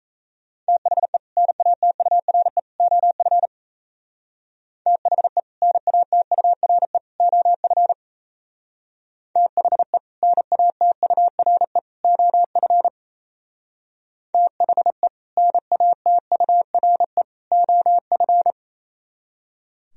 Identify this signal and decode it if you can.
Morse code, heard as a keyed tone, about 36 wpm, 700 Hz. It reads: THE NATURE OF THE NATURE OF THE NATURE OF THE NATURE OF